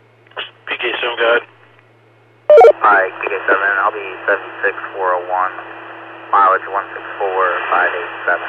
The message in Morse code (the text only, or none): none